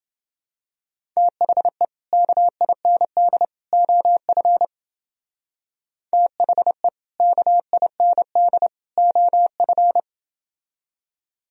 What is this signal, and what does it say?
Morse code, heard as a keyed tone, about 29 wpm, 700 Hz. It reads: THE KIND OF THE KIND OF